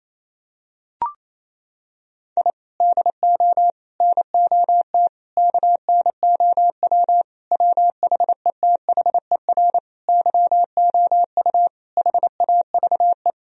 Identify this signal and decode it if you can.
Morse code, heard as a keyed tone, about 28 wpm, 700 Hz. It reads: I DO NOT KNOW WHETHER YOU HAVE
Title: Sentences from Top 1000 Words - Review Part 3 - 28wpm